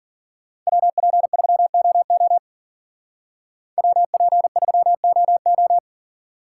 Morse code, heard as a keyed tone, about 50 wpm, 700 Hz. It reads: WP3OO WP3OO
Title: Speed Racing - USA 2x2 Call Signs 40wpm